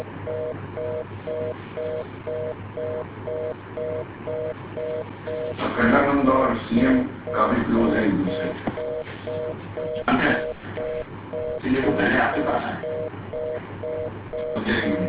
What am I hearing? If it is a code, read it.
Telephone call progress tone (North America), heard as a telephone network's tone, about 480, 620 Hz, with speech. Reorder tone.